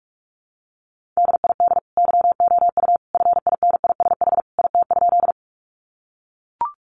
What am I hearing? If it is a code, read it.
Morse code, heard as a keyed tone, about 44 wpm, 700 Hz. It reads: DID YOU FINISH IT?
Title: Sentences from Top 1200 Words 45wpm